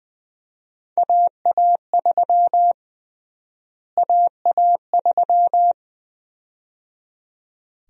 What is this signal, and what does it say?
Morse code, heard as a keyed tone, about 20 wpm, 700 Hz. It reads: AA3 AA3
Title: USA Call Sign Prefixes - Repeated 2x - 20wpm